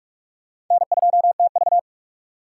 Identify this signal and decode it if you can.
Morse code, heard as a keyed tone, about 47 wpm, 700 Hz. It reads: N2TV
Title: USA Call Signs - 4 Characters - 6x Character Spacing - 45wpm